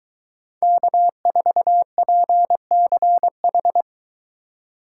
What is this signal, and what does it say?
Morse code, heard as a keyed tone, about 22 wpm, 700 Hz. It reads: K4PCH